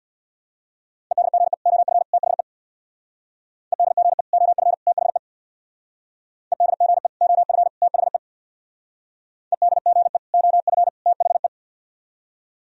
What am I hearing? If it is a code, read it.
Morse code, heard as a keyed tone, about 63 wpm, 700 Hz. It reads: EDGE OF THE EDGE OF THE EDGE OF THE EDGE OF THE